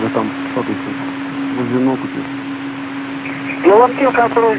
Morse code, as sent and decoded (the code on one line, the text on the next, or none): none